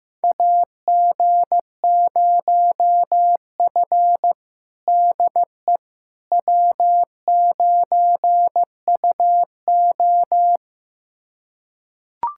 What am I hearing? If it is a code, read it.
Morse code, heard as a keyed tone, about 15 wpm, 700 Hz. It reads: AG0F DE W9UO
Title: USA Callsign DE Callsign - 4 Characters - 15wpm